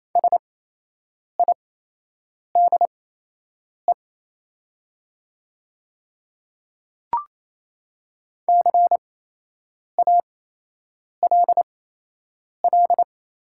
Morse code, heard as a keyed tone, about 27 wpm, spaced as for 8 wpm, 700 Hz. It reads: SIDE CALL
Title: 8x Character Spacing - Top 100 Words - 28wpm